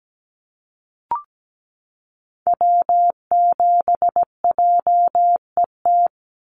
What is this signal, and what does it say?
Morse code, heard as a keyed tone, about 17 wpm, 700 Hz. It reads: W7JET
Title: NA SOTA Activator Callsigns 17wpm